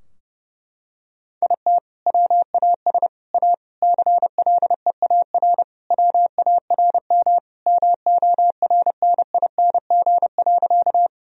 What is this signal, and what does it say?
Morse code, heard as a keyed tone, about 30 wpm, 700 Hz. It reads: IT WAS A CLEAR WARM MORNING.
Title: Sentences from Top 1000 Words - Review Part 2 - 30wpm